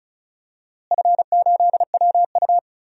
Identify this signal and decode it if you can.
Morse code, heard as a keyed tone, about 36 wpm, 700 Hz. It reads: F8WU
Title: DXCC-100 Call Signs - 4 Characters - Repeated 3x - 35wpm